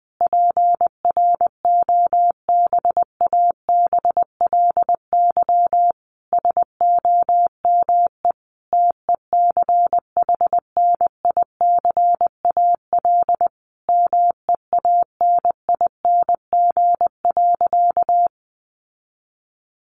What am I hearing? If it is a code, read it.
Morse code, heard as a keyed tone, about 20 wpm, 700 Hz. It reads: PROBABLY SOME TECHNICAL MEANING.
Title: Sentences from Top 1100 Words 20wpm